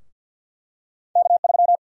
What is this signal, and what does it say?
Morse code, heard as a keyed tone, about 52 wpm, 700 Hz. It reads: K3
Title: USA Call Sign Prefixes - Repeated 2x - 50wpm